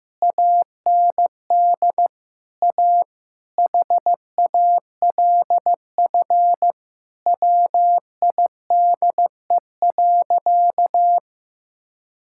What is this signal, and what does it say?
Morse code, heard as a keyed tone, about 15 wpm, 700 Hz. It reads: AND A HALF WIDE.
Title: Sentences from Top 1000 Words - Review Part 6 - 15wpm